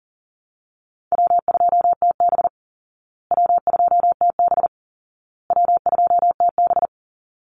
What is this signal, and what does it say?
Morse code, heard as a keyed tone, about 40 wpm, 700 Hz. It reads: W2TB W2TB W2TB
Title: Call Signs Repeated 2x - 40wpm